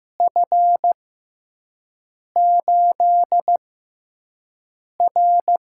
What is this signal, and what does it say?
Morse code, heard as a keyed tone, about 15 wpm, spaced as for 5 wpm, 700 Hz. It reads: F8R
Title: USA Call Signs - 6x Character Spacing - 15wpm